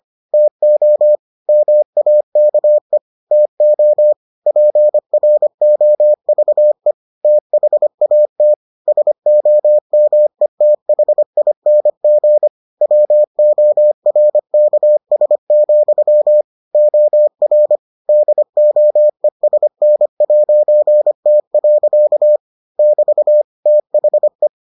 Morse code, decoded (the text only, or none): TO MAKE TO PROVE THAT SOMETHING WORKS, OR DOESN'T. <BT> THE